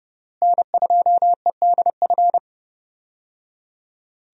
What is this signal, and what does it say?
Morse code, heard as a keyed tone, about 30 wpm, 700 Hz. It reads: N2EDF
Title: USA Call Signs - 5 Characters - Repeated 3x - 30wpm